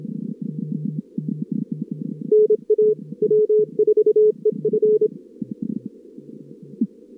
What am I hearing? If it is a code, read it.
Morse code, heard as a keyed tone, about 26 wpm, 440 Hz. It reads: NA W4EF
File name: CWSprint / Feb2022 / k5tr-feb-na-sprint-cw-k5ot -0100